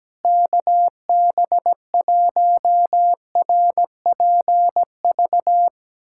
Morse code, heard as a keyed tone, about 17 wpm, 700 Hz. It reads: KB1RPV
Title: USA Callsign DE Callsign - 6 Characters - 17wpm